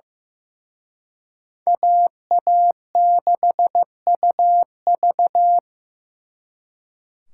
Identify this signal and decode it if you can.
Morse code, heard as a keyed tone, about 18 wpm, 700 Hz. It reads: AA6UV
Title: Speed Racing - USA 2x2 Call Signs 15wpm